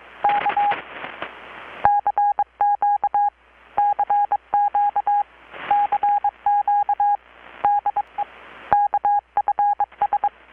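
Morse code, heard as a keyed tone, about 22 wpm, 820 Hz. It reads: K CQ CQ CQ DE KFS
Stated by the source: Morse from KPH's Night of Nights XX maritime radiotelegraphy event, received in Chiapas, Mexico about 3,800 km away on a Kenwood R-600 with a grounded 10 m long-wire antenna.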